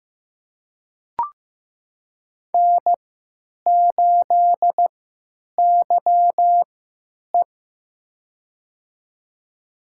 Morse code, heard as a keyed tone, about 15 wpm, spaced as for 9 wpm, 700 Hz. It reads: N8YE
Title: USA Call Signs - 4 Characters - 3x Character Spacing - 15wpm